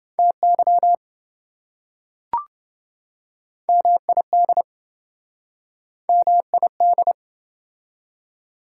Morse code, heard as a keyed tone, about 28 wpm, 700 Hz. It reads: TY MID MID